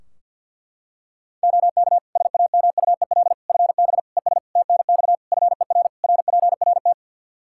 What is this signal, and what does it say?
Morse code, heard as a keyed tone, about 49 wpm, 700 Hz. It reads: OK SAMUEL FB ES TNX FER RPRT
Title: Practice QSO - Exchange 2 - Info - 50wpm